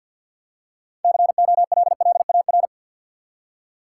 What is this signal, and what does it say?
Morse code, heard as a keyed tone, about 53 wpm, 700 Hz. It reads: COLLAR
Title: Six Letter Words - 50wpm